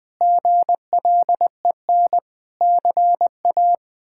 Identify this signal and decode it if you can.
Morse code, heard as a keyed tone, about 20 wpm, 700 Hz. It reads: GLEN CA
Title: K1USN SST QSO Practice - 20wpm